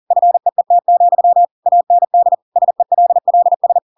Morse code, heard as a keyed tone, about 40 wpm, 700 Hz. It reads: FEET, AND SELLS